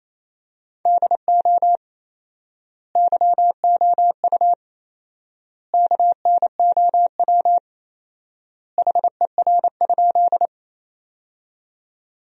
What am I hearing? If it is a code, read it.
Morse code, heard as a keyed tone, about 28 wpm, 700 Hz. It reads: DO YOU KNOW HER?
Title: 4x Word-Spacing - Sentences from Top 100 Words - 28wpm